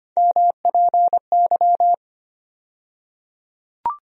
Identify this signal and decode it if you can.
Morse code, heard as a keyed tone, about 25 wpm, 700 Hz. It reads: MPY
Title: USA Call Sign Postfixes - 3 Letters 25wpm